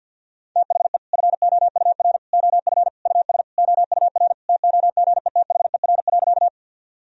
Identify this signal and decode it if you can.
Morse code, heard as a keyed tone, about 52 wpm, 700 Hz. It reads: THE FOUR OF US OUR TOGETHER.
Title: Sentences from Top 1000 Words - Review Part 1 - 50wpm